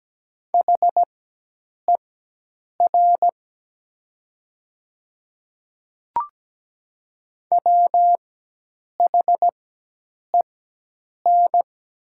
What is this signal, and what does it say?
Morse code, heard as a keyed tone, about 17 wpm, spaced as for 8 wpm, 700 Hz. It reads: HER WHEN
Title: 4x Character Spacing - Top 200 Words - 17wpm